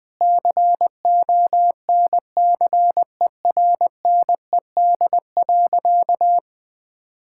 Morse code, heard as a keyed tone, about 20 wpm, 700 Hz. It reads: CONCERNED.
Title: Sentences from Top 1000 Words - Review Part 6 - 20wpm